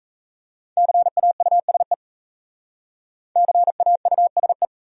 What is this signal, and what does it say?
Morse code, heard as a keyed tone, about 34 wpm, 700 Hz. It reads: CAUSE CAUSE
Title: Speed-Racing Top 200 Words 28wpm